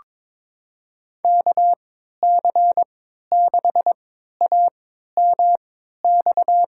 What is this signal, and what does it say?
Morse code, heard as a keyed tone, about 22 wpm, spaced as for 13 wpm, 700 Hz. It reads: KC6AMX
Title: USA Call Signs - 6 Characters - 3x Character Spacing - 22wpm